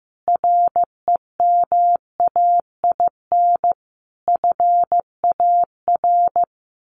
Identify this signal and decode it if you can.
Morse code, heard as a keyed tone, about 14 wpm, 700 Hz. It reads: REMAIN FAR
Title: Sentences from Top 1400 Words 15wpm